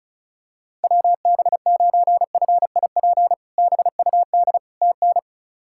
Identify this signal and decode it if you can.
Morse code, heard as a keyed tone, about 34 wpm, 700 Hz. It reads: WB9FIP BUD TN
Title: K1USN SST QSO Practice - 35wpm